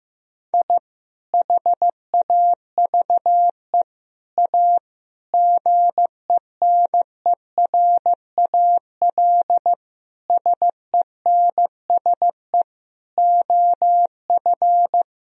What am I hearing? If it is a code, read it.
Morse code, heard as a keyed tone, about 15 wpm, 700 Hz. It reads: I HAVE A GENERAL SENSE OF
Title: Sentences from Top 1000 Words - Review Part 4 - 15wpm